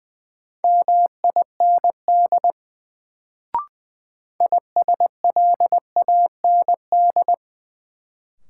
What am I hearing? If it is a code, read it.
Morse code, heard as a keyed tone, about 21 wpm, 700 Hz. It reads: MIND ISLAND
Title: Sentences from Top 500 Words - New Word Drill - 20wpm